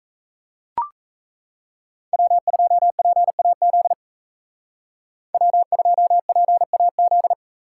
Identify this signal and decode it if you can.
Morse code, heard as a keyed tone, about 36 wpm, 700 Hz. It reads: W2PAZ W2PAZ